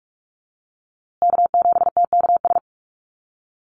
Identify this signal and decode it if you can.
Morse code, heard as a keyed tone, about 44 wpm, 700 Hz. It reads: K7TKS
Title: USA Call Signs Repeated 3x 45wpm